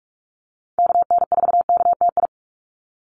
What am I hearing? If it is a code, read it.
Morse code, heard as a keyed tone, about 44 wpm, 700 Hz. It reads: KN4KTI
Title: USA Call Signs 45wpm